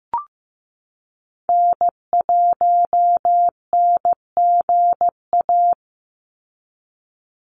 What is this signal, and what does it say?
Morse code, heard as a keyed tone, about 15 wpm, 700 Hz. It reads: N1NGA
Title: USA Call Signs 15wpm